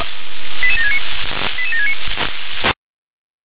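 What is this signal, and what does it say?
Five-tone selective call (ZVEI), heard as quick runs of five tones, tones of about 70 ms. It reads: 88769 88769
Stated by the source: ZVEI five-tone selective call received on an SDR.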